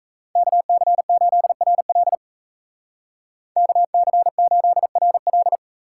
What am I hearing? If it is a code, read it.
Morse code, heard as a keyed tone, about 36 wpm, 700 Hz. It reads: KC8RL KC8RL